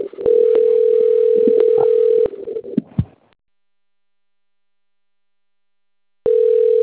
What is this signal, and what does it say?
Ringback tone (North America), 440 + 480 Hz.
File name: rg-1600-unknown- - - .17348